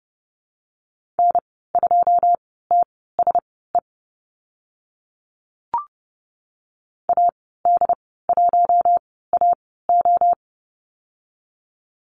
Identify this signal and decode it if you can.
Morse code, heard as a keyed tone, about 30 wpm, spaced as for 17 wpm, 700 Hz. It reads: N2TSE AD1AO
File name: Lesson 14g - Number 2 - Call Signs 30wpm